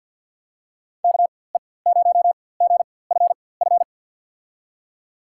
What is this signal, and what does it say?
Morse code, heard as a keyed tone, about 51 wpm, spaced as for 23 wpm, 700 Hz. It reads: KE0GFF
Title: USA Call Signs - 4x Character Spacing - 50wpm